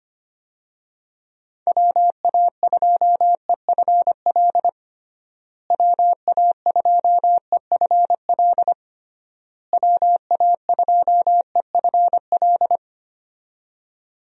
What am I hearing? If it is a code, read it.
Morse code, heard as a keyed tone, about 25 wpm, 700 Hz. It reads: WA2EFL WA2EFL WA2EFL
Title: USA Call Signs - 6 Characters - Repeated 3x - 25wpm